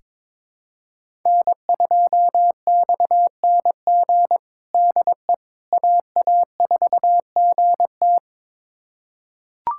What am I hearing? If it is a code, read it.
Morse code, heard as a keyed tone, about 22 wpm, 700 Hz. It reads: N2XNG DE AA4GT
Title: USA Callsign DE Callsign - 5 Characters - 22wpm